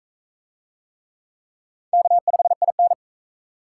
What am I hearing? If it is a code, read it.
Morse code, heard as a keyed tone, about 39 wpm, 700 Hz. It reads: K5IN